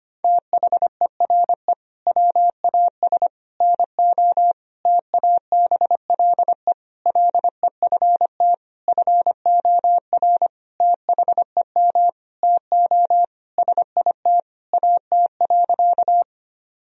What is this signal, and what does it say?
Morse code, heard as a keyed tone, about 25 wpm, 700 Hz. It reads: THERE WAS NO TABLE LEFT FOR THEM TO SIT AT.
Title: Sentences from Top 1000 Words - Review Part 2 - 25wpm